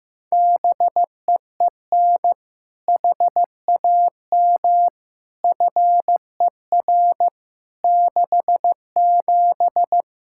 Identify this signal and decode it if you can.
Morse code, heard as a keyed tone, about 15 wpm, 700 Hz. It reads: BEEN HAM FER 67
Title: Practice QSO - Exchange 3 - More Info - 15wpm